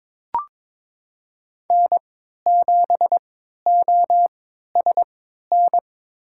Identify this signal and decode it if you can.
Morse code, heard as a keyed tone, about 22 wpm, spaced as for 12 wpm, 700 Hz. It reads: N7OSN
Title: USA Call Signs - 5 Characters - 3x Character Spacing - 22wpm